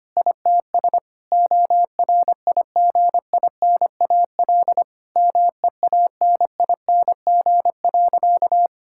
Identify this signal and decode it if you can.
Morse code, heard as a keyed tone, about 25 wpm, 700 Hz. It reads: ITS ORIGINAL MEANING.
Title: Sentences from Top 1000 Words - Review Part 6 - 25wpm